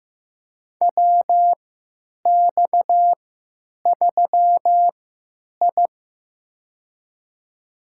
Morse code, heard as a keyed tone, about 15 wpm, spaced as for 9 wpm, 700 Hz. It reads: WX3I
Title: USA Call Signs - 4 Characters - 3x Character Spacing - 15wpm